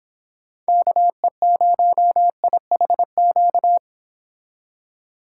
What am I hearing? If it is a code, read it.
Morse code, heard as a keyed tone, about 30 wpm, 700 Hz. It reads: KE0IHQ